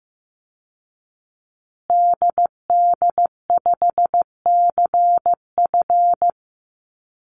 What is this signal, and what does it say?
Morse code, heard as a keyed tone, about 15 wpm, 700 Hz. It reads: DD5CF
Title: Call Signs Repeated 2x - 15wpm